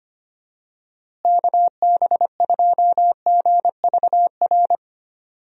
Morse code, heard as a keyed tone, about 25 wpm, 700 Hz. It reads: KB2GVR
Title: USA Call Signs - 6 Characters - Repeated 3x - 25wpm